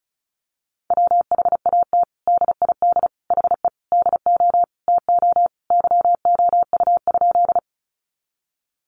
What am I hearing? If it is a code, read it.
Morse code, heard as a keyed tone, about 34 wpm, 700 Hz. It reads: WHAT DID HE DO TO YOU?
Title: Sentences from Top 100 Words 35wpm